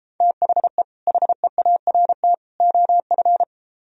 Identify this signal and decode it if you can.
Morse code, heard as a keyed tone, about 32 wpm, 700 Hz. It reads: THE HEART OF